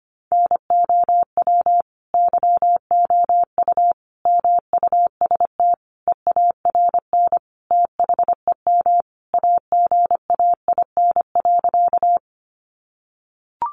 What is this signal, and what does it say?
Morse code, heard as a keyed tone, about 25 wpm, 700 Hz. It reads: NOW YOU MUST EARN THEM AGAIN.
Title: Sentences from Top 1500 Words 25wpm